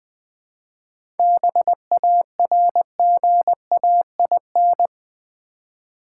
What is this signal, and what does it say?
Morse code, heard as a keyed tone, about 20 wpm, 700 Hz. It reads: BARGAIN
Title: 7 Letter Words - First 3 Letters Given - 20wpm